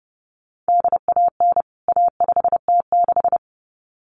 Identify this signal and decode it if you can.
Morse code, heard as a keyed tone, about 30 wpm, 700 Hz. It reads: DAN A5T6
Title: CWT Exchange 30wpm